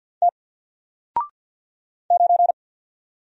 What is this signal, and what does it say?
Morse code, heard as a keyed tone, about 49 wpm, 700 Hz. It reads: T 9
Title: Single Letter-Number Alt 50wpm